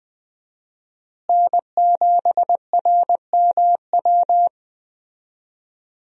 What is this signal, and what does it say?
Morse code, heard as a keyed tone, about 20 wpm, 700 Hz. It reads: N7RMW
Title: USA Call Signs - 5 Characters - Repeated 3x - 20wpm